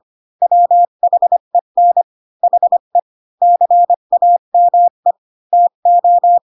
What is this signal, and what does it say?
Morse code, heard as a keyed tone, about 25 wpm, 700 Hz. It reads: WHEN HE CAME TO
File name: Pinocchio / Pinocchio ch5 20 25 ik1hge